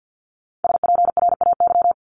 Morse code, heard as a keyed tone, about 50 wpm, 700 Hz. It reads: SPRAY
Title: Multi-letter Phonemes 50wpm